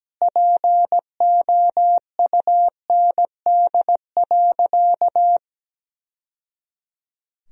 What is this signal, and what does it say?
Morse code, heard as a keyed tone, about 17 wpm, 700 Hz. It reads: POUND.
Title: Sentences from Top 1000 Words - Review Part 2 - 17wpm